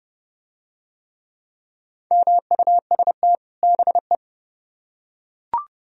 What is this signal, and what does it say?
Morse code, heard as a keyed tone, about 29 wpm, 700 Hz. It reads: MUST BE